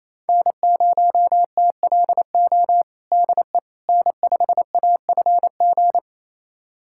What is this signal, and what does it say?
Morse code, heard as a keyed tone, about 27 wpm, 700 Hz. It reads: N0TLO DE N5AFG
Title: USA Callsign DE Callsign - 5 Characters - 28wpm